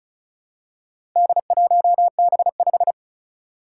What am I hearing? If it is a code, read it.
Morse code, heard as a keyed tone, about 36 wpm, 700 Hz. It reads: D1B5
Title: Four Challenging Letter-Number Combinations 35wpm